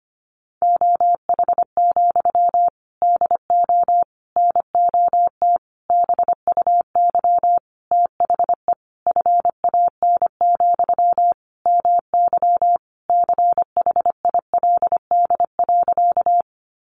Morse code, heard as a keyed tone, about 25 wpm, 700 Hz. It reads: OH, DO NOT BUY THE FAN, MY CHILD.
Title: Sentences from Top 1300 Words 25wpm